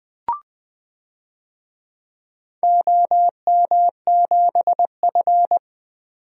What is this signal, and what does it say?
Morse code, heard as a keyed tone, about 20 wpm, 700 Hz. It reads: OM7F
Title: DXCC-100 Call Signs - 4 Characters - 20wpm